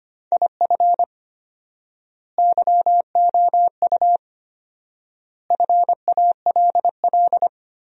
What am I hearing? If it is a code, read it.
Morse code, heard as a keyed tone, about 25 wpm, 700 Hz. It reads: IF YOU FALL
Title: 4x Word-Spacing - Sentences from Top 500 Words - 25wpm